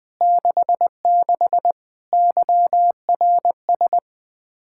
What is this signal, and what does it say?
Morse code, heard as a keyed tone, about 21 wpm, 700 Hz. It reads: 66 YRS
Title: K1USN SST QSO Practice 20wpm